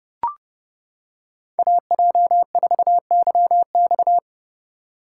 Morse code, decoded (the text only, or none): AJ4YX